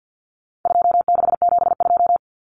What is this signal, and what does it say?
Morse code, heard as a keyed tone, about 50 wpm, 700 Hz. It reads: 2672
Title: Numbers - 4-Digit 50wpm